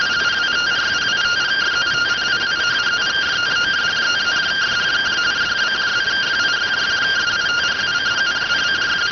Two-tone teleprinter signal, about 170 Hz shift.